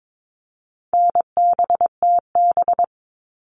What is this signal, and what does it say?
Morse code, heard as a keyed tone, about 22 wpm, 700 Hz. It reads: NBTB
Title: All Cut Numbers - 4-Digit 22wpm